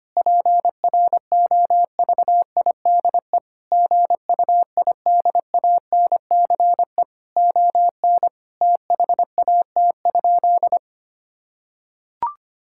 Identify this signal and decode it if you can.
Morse code, heard as a keyed tone, about 25 wpm, 700 Hz. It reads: PROVIDE GUIDANCE ON THAT?
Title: Sentences from Top 1600 Words 25wpm